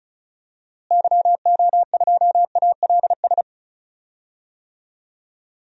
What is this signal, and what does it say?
Morse code, heard as a keyed tone, about 36 wpm, 700 Hz. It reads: YO2ALS
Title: DXCC-100 Call Signs - 6 Characters - Repeated 3x - 35wpm